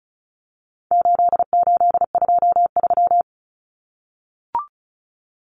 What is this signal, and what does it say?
Morse code, heard as a keyed tone, about 34 wpm, 700 Hz. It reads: 8823
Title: Numbers - 4-Digit 35wpm